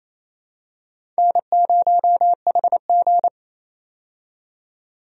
Morse code, heard as a keyed tone, about 28 wpm, 700 Hz. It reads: N0HG
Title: USA Call Signs - 4 Characters - 28wpm